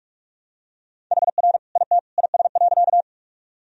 Morse code, heard as a keyed tone, about 46 wpm, 700 Hz. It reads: HR IT IS.
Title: Sentences from Top 200 Words and Abbreviations 45wpm